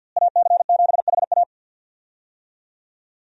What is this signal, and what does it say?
Morse code, heard as a keyed tone, about 51 wpm, 700 Hz. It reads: AC7HA
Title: USA Call Signs - 5 Characters - Repeated 2x - 50wpm